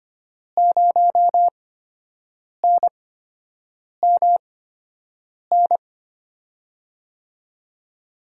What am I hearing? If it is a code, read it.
Morse code, heard as a keyed tone, about 25 wpm, spaced as for 7 wpm, 700 Hz. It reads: 0NMN
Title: USA Call Signs - 6 Characters - 8x Character Spacing - 25wpm